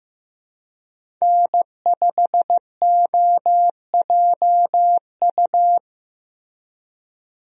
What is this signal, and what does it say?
Morse code, heard as a keyed tone, about 14 wpm, 700 Hz. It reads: N5OJU
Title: USA Call Signs - 5 Characters - Repeated 2x - 15wpm